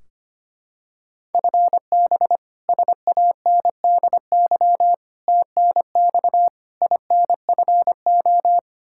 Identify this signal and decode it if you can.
Morse code, heard as a keyed tone, about 25 wpm, 700 Hz. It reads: FB SANDY TNX INFO
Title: Practice QSO - Exchange 3 - More Info - 25wpm